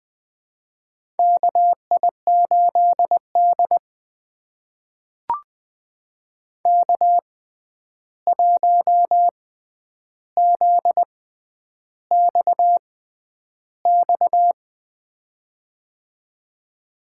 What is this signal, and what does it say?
Morse code, heard as a keyed tone, about 20 wpm, spaced as for 10 wpm, 700 Hz. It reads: KI8D K1ZXX
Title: USA Call Signs - 6x Character Spacing - 20wpm